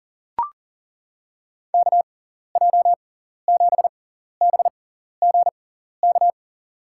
Morse code, heard as a keyed tone, about 40 wpm, spaced as for 14 wpm, 700 Hz. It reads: KJ7BGK
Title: USA Call Signs - 6 Characters - 6x Character Spacing - 40wpm